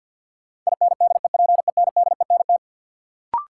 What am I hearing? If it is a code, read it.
Morse code, heard as a keyed tone, about 47 wpm, 700 Hz. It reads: INDEPENDENT
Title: Sentences from Top 900 Words - New Word Drill - 50wpm